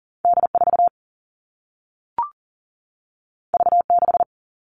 Morse code, heard as a keyed tone, about 41 wpm, 700 Hz. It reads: D4 V6
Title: All Cut Numbers - 2-Digit 40wpm